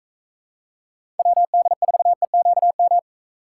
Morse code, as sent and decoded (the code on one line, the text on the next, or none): .-- -.. ....- . --.- --
WD4EQM